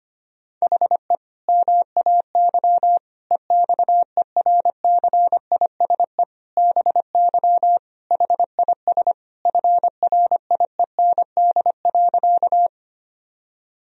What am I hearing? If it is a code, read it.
Morse code, heard as a keyed tone, about 25 wpm, 700 Hz. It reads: HE MAY EXERCISE BY HIS FRIEND.
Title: Sentences from Top 1000 Words - Review Part 4 - 25wpm